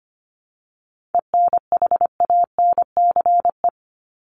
Morse code, heard as a keyed tone, about 25 wpm, 700 Hz. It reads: ENHANCE
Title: Top 1251-1500 Words 25wpm